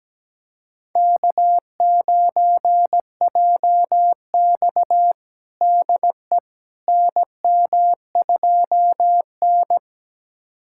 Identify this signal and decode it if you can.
Morse code, heard as a keyed tone, about 17 wpm, 700 Hz. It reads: K9JX DE NM2N
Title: USA Callsign DE Callsign - 4 Characters - 17wpm